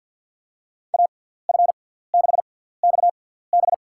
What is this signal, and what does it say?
Morse code, heard as a keyed tone, about 53 wpm, spaced as for 17 wpm, 690 Hz. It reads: AF6XB
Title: USA Call Signs - 6x Character Spacing - 50wpm